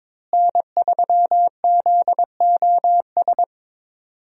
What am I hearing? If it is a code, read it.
Morse code, heard as a keyed tone, about 22 wpm, 700 Hz. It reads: N3ZOS
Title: USA Call Signs - 5 Characters - Repeated 2x - 22wpm